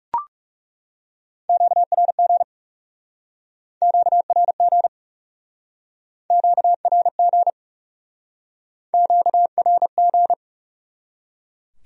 Morse code, heard as a keyed tone, about 40 wpm, 700 Hz. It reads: QRG QRG QRG QRG